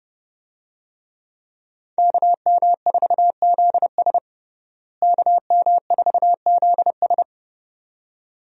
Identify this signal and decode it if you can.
Morse code, heard as a keyed tone, about 30 wpm, 700 Hz. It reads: KM4ZS KM4ZS
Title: USA Call Signs - 5 Characters - Repeated 2x - 30wpm